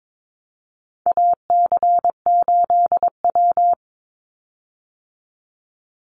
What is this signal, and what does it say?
Morse code, heard as a keyed tone, about 22 wpm, 700 Hz. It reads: AC8W
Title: Call Signs Repeated 2x - 22wpm